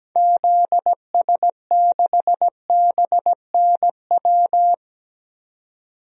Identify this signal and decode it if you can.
Morse code, heard as a keyed tone, about 17 wpm, 700 Hz. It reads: ZS6BNW
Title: DXCC-100 Call Signs - 6 Characters - 17wpm